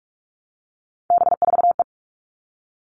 Morse code, heard as a keyed tone, about 44 wpm, 700 Hz. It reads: B4E
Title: All Cut Numbers - 3-Digit 45wpm